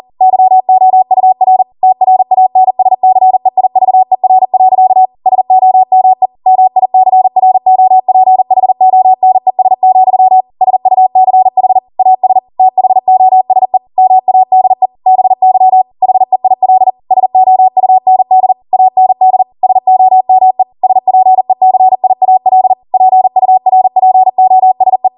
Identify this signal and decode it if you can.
Morse code, heard as a keyed tone, about 40 wpm, 750 Hz. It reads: YOUR TRANSCEIVER. SOME MICROPHONES, SUCH AS THOSE MADE BY HEIL SOUND AND SOME SPECIAL PURPOSE